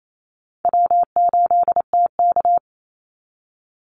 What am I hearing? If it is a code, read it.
Morse code, heard as a keyed tone, about 28 wpm, 700 Hz. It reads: W8TK
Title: Easy Call Signs - 28wpm